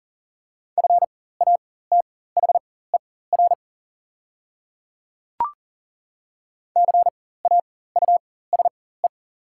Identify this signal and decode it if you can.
Morse code, heard as a keyed tone, about 39 wpm, spaced as for 19 wpm, 700 Hz. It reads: FATHER CAUSE
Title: 4x Character Spacing - Top 300 Words - 40wpm